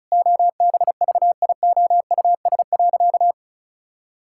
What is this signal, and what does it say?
Morse code, heard as a keyed tone, about 34 wpm, 700 Hz. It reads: OBVIOUS.
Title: Sentences from Top 1600 Words 35wpm